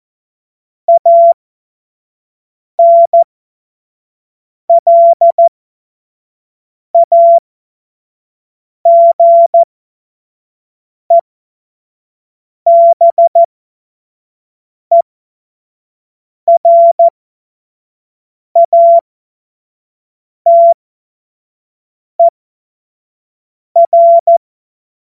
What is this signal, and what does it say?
Morse code, heard as a keyed tone, about 14 wpm, spaced as for 5 wpm, 680 Hz. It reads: ANLAGEBERATER
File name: Morsekurs / Witze / Witz 27. Juni 2023 25 F70